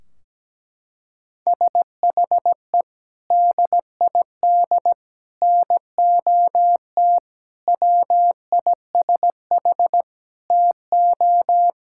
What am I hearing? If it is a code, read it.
Morse code, heard as a keyed tone, about 17 wpm, 700 Hz. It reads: SHE DID NOT WISH TO
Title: Sentences from Top 1600 Words 17wpm